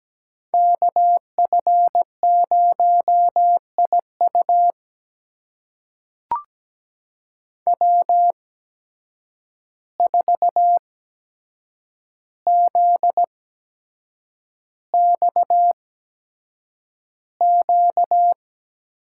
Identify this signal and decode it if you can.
Morse code, heard as a keyed tone, about 17 wpm, spaced as for 7 wpm, 700 Hz. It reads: KF0IU W4ZXQ
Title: USA Call Signs - 5 Characters - 8x Character Spacing - 17wpm